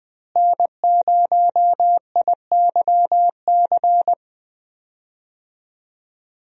Morse code, heard as a keyed tone, about 20 wpm, 700 Hz. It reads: N0IYC
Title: USA Call Signs - 5 Characters - Repeated 3x - 20wpm